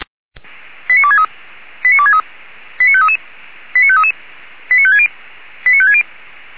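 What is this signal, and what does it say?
Five-tone selective call (ZVEI), heard as quick runs of five tones, tones of about 70 ms. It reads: 89263 89263 89533 89533 89560 89560
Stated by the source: Five-tone ZVEI selective call received by SDR, from audio.f85360.feuerwehr-orlamuende.org, a fire brigade (Feuerwehr Orlamünde).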